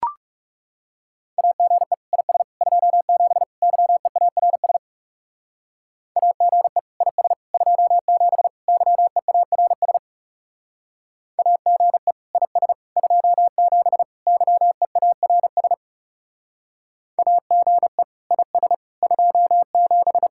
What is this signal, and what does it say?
Morse code, heard as a keyed tone, about 34 wpm, 700 Hz. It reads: AGE IS 27 YEARS AGE IS 27 YEARS AGE IS 27 YEARS AGE IS 27
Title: Speed-Racing Common QSO Phrases Encore - 30wpm